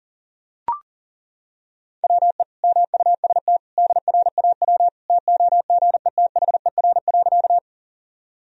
Morse code, heard as a keyed tone, about 40 wpm, 700 Hz. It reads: WE MUST DRAW TOGETHER.
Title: Sentences from Top 1000 Words - Review Part 1 - 40wpm